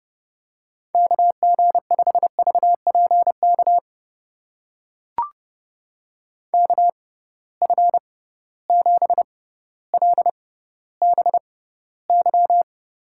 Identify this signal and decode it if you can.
Morse code, heard as a keyed tone, about 30 wpm, spaced as for 15 wpm, 700 Hz. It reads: KG5VPK KF7LBY
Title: USA Call Signs - 6 Characters - 6x Character Spacing - 30wpm